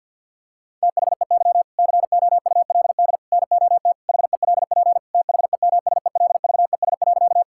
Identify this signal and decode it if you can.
Morse code, heard as a keyed tone, about 50 wpm, 700 Hz. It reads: THEY COULD NOT HELP THEMSELVES.
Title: Sentences from Top 1000 Words - Review Part 5 - 50wpm